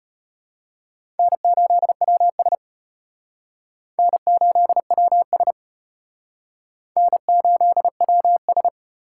Morse code, heard as a keyed tone, about 34 wpm, 700 Hz. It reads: N8WS N8WS N8WS